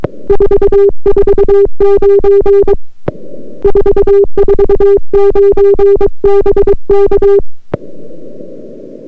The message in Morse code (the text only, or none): 449 449 BK